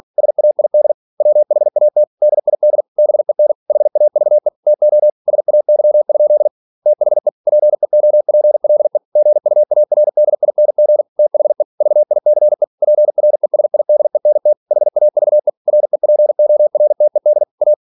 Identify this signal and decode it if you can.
Morse code, heard as a keyed tone, about 46 wpm, 600 Hz. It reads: SAID WHAT DID BEN HAVE TO SAY? THE PEOPLE GUARDING THE VICE PRESIDENT HAVE REPORTED A
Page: Never 47wpm